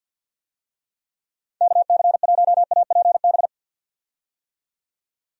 Morse code, heard as a keyed tone, about 47 wpm, 700 Hz. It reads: KC1APB
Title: USA Call Signs - 6 Characters - 50wpm